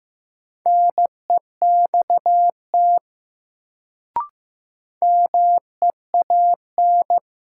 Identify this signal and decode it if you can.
Morse code, heard as a keyed tone, about 15 wpm, 700 Hz. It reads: NEXT MEAN
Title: Sets of 2 Words - New Word Drill - 15wpm